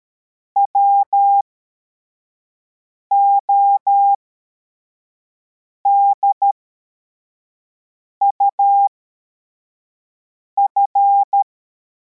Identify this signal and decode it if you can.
Morse code, heard as a keyed tone, about 13 wpm, spaced as for 4 wpm, 800 Hz. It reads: WODUF
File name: formation / cw / Text62